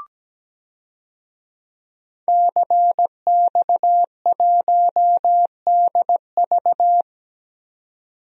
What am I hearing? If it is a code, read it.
Morse code, heard as a keyed tone, about 17 wpm, 700 Hz. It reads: CX1DV
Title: DXCC-100 Call Signs - 5 Characters - 17wpm